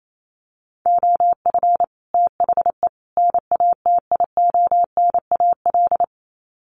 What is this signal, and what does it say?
Morse code, heard as a keyed tone, about 28 wpm, 700 Hz. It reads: OF THE NATIONAL
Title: Sets of 3 Words 28wpm